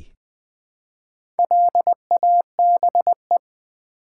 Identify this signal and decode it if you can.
Morse code, heard as a keyed tone, about 20 wpm, 700 Hz. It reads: LABE
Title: Build A Word - 1 Letter At A Time - 5-Letter Words - 20wpm